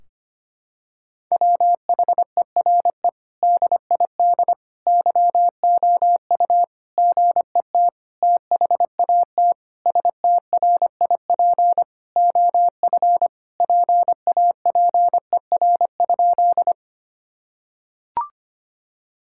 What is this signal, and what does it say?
Morse code, heard as a keyed tone, about 25 wpm, 700 Hz. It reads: WHERE DID YOU GET THAT STRIP OF PAPER?
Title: Sentences from Top 1600 Words 25wpm